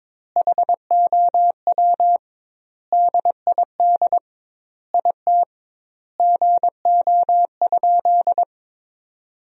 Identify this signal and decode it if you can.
Morse code, heard as a keyed tone, about 22 wpm, 700 Hz. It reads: HOW DID IT GO?
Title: 2x Word-Spacing - Sentences from Top 200 Words - 22wpm